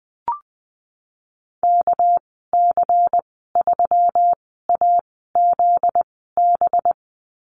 Morse code, heard as a keyed tone, about 20 wpm, spaced as for 14 wpm, 700 Hz. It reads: KC3AZB
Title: 245 Lesson 39g - Letter z - Call Signs 20wpm